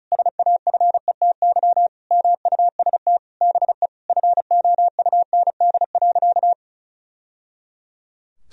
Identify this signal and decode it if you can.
Morse code, heard as a keyed tone, about 34 wpm, 700 Hz. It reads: SAFETY MUST BE FOUND.
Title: Sentences from Top 1000 Words - Review Part 4 - 35wpm